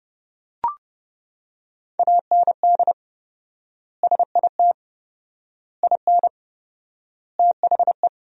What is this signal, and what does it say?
Morse code, heard as a keyed tone, about 32 wpm, 700 Hz. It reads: AND SIT IN THE
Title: 4x Word-Spacing - Sentences from Top 500 Words - 30wpm